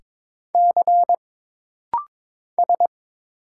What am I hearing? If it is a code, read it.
Morse code, heard as a keyed tone, about 22 wpm, 700 Hz. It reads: C S
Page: This practice set is composed of Letters, Numbers, common punctuations, and common prosigns.